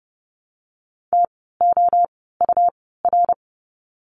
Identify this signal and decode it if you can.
Morse code, heard as a keyed tone, about 30 wpm, spaced as for 17 wpm, 700 Hz. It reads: TOUR
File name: Lesson 17f - Letter u - Words 30wpm